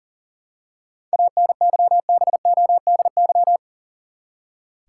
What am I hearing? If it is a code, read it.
Morse code, heard as a keyed tone, about 39 wpm, 700 Hz. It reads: ANYBODY
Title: Seven Letter Words - 40wpm